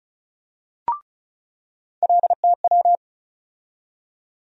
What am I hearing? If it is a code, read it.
Morse code, heard as a keyed tone, about 35 wpm, 700 Hz. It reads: LTW
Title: USA Call Sign Postfixes - 3 Letters 35wpm